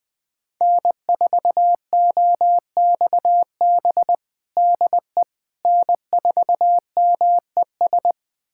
Morse code, heard as a keyed tone, about 20 wpm, 700 Hz. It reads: N4OXB DE N4MES
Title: USA Callsign DE Callsign - 5 Characters - 20wpm